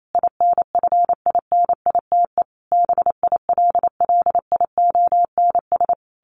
Morse code, heard as a keyed tone, about 28 wpm, 700 Hz. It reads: INFINITE BILLIONS
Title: A Sound Of Thunder 28wpm